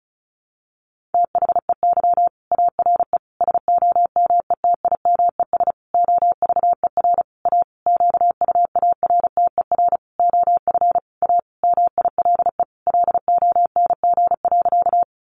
Morse code, heard as a keyed tone, about 34 wpm, 700 Hz. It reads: THEY ARE SOMETIMES OVER A QUARTER OF A MILE LONG.
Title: Sentences from Top 1300 Words 35wpm